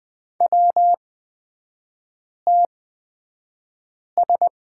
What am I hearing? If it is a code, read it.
Morse code, heard as a keyed tone, about 20 wpm, spaced as for 5 wpm, 690 Hz. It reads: WTS
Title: 229 Lesson 37e - Letter x - Three Characters 20wpm